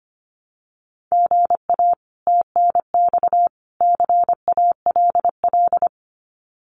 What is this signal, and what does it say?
Morse code, heard as a keyed tone, about 25 wpm, 700 Hz. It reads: GA TNX CALL
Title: Common QSO Phrases 25wpm